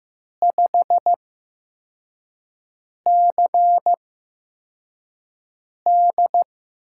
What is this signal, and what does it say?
Morse code, heard as a keyed tone, about 15 wpm, spaced as for 4 wpm, 700 Hz. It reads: 5CD
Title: USA Call Signs - 5 Characters - 8x Character Spacing - 15wpm